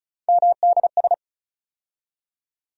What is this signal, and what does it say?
Morse code, heard as a keyed tone, about 36 wpm, 700 Hz. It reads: MDS
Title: USA Call Sign Postfixes - 3 Letters 35wpm